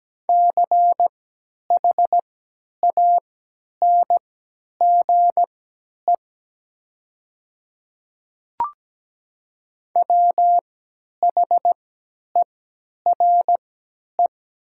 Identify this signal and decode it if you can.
Morse code, heard as a keyed tone, about 17 wpm, spaced as for 10 wpm, 700 Hz. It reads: CHANGE WHERE
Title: 3x Character Spacing - Top 300 Words - 17wpm